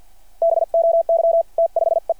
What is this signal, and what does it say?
Morse code, heard as a keyed tone, about 52 wpm, spaced as for 45 wpm, 660 Hz. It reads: BOY T5E
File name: archived / CRSnet